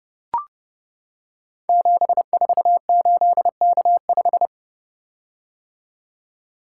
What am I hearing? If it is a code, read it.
Morse code, heard as a keyed tone, about 29 wpm, 700 Hz. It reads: 748K5
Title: Code Groups - 5 Letter:Numbers - 30wpm